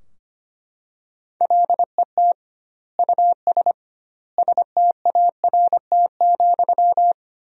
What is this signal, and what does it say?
Morse code, heard as a keyed tone, about 25 wpm, 700 Hz. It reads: LET US START,
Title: 2x Word-Spacing - Sentences from Top 300 Words - 25wpm